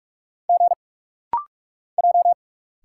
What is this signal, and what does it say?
Morse code, heard as a keyed tone, about 46 wpm, 700 Hz. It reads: G J
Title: Single Letters - Rapid-Fire 45wpm